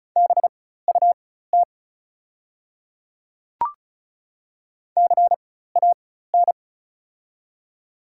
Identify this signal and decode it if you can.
Morse code, heard as a keyed tone, about 36 wpm, spaced as for 16 wpm, 700 Hz. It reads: BUT CAN
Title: 4x Character Spacing - Top 100 Words - 35wpm